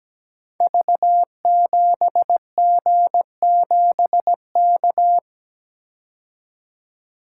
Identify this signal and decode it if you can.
Morse code, heard as a keyed tone, about 17 wpm, 700 Hz. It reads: V7G7K
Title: Code Groups - 5 Letter:Numbers - 17wpm